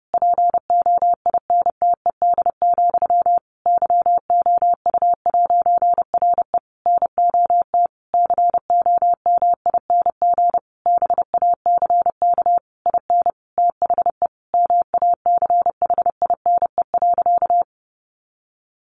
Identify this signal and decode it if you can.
Morse code, heard as a keyed tone, about 30 wpm, 700 Hz. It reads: POINTED, YOU'RE NOT COMING BACK IN THE MACHINE.
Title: A Sound Of Thunder 30wpm